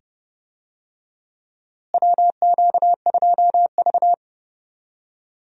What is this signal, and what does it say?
Morse code, heard as a keyed tone, about 30 wpm, 700 Hz. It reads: WQ2V
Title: USA Call Signs - 4 Characters - Repeated 2x - 30wpm